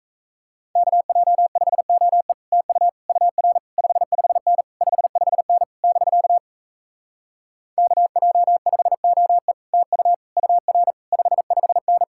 Morse code, heard as a keyed tone, about 38 wpm, 700 Hz. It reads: KJ5OE TU UR 55N 55N <BK> KJ5OE TU UR 55N